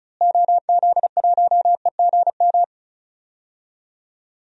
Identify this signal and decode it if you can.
Morse code, heard as a keyed tone, about 34 wpm, 700 Hz. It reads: OZ1EGM
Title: DXCC-100 Call Signs - 6 Characters - Repeated 3x - 35wpm